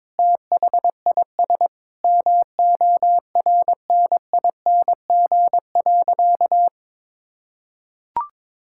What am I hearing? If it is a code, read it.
Morse code, heard as a keyed tone, about 22 wpm, 700 Hz. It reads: THIS MORNING.
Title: Sentences from Top 1000 Words - Review Part 6 - 22wpm